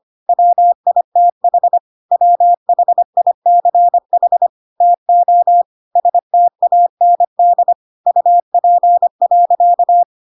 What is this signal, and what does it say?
Morse code, heard as a keyed tone, about 25 wpm, 700 Hz. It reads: WITH WHICH TO STAND UP.